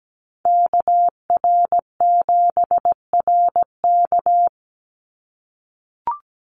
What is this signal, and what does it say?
Morse code, heard as a keyed tone, about 17 wpm, 700 Hz. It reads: KR7RK
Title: NA SOTA Chaser Callsigns 17wpm